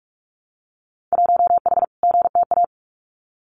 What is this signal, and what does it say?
Morse code, heard as a keyed tone, about 46 wpm, 700 Hz. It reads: 1H GTA
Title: Winter Field Day Exchanges 45wpm